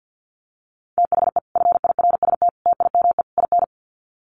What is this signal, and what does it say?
Morse code, heard as a keyed tone, about 50 wpm, 700 Hz. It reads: THE FIRST TIME IN
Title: Sets of 4 Words 50wpm